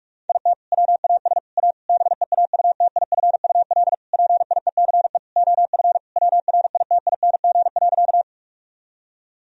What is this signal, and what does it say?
Morse code, heard as a keyed tone, about 46 wpm, 700 Hz. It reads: IT WAS A BEAUTIFUL PIECE OF WRITING.
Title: Sentences from Top 1000 Words - Review Part 2 - 45wpm